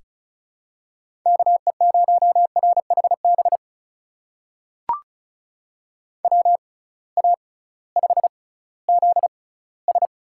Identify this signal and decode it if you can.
Morse code, heard as a keyed tone, about 36 wpm, spaced as for 19 wpm, 700 Hz. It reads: KE0RHB WA5ZS
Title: USA Call Signs - 6 Characters - 6x Character Spacing - 35wpm